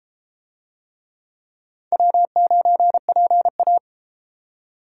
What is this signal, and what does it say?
Morse code, heard as a keyed tone, about 32 wpm, 700 Hz. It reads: W9PA